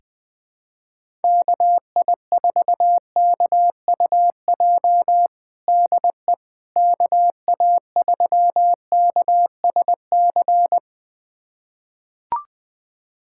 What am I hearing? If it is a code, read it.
Morse code, heard as a keyed tone, about 20 wpm, 700 Hz. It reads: KI4KUJ DE KA3KSC
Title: USA Callsign DE Callsign 20wpm